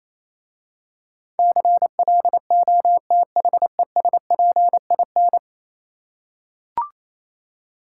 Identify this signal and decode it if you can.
Morse code, heard as a keyed tone, about 28 wpm, 700 Hz. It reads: CLOTHESPIN